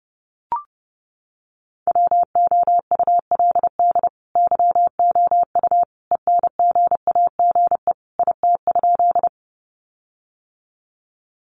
Morse code, heard as a keyed tone, about 30 wpm, 700 Hz. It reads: WOULD YOU ENGAGE IT?
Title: Sentences from Top 1500 Words 30wpm